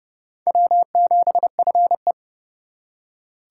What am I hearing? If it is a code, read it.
Morse code, heard as a keyed tone, about 35 wpm, 700 Hz. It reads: W7FE
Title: Speed Racing - USA 1x2 Call Signs 30wpm